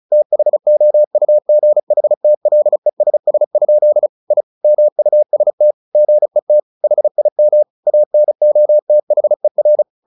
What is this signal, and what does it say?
Morse code, heard as a keyed tone, about 34 wpm, 600 Hz. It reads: THOUGHTLESS? I MUST GET HIM ANOTHER
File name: the old man and the sea / the old man and the sea 1 35wpm